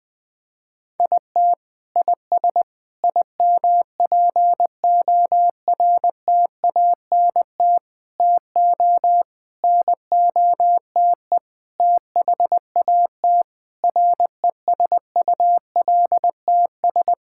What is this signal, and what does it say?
Morse code, heard as a keyed tone, about 20 wpm, 700 Hz. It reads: IT IS IMPORTANT TO NOTE THAT RESULTS
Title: Sentences with 6-Word Phrases - Spoken Beginning - 20wpm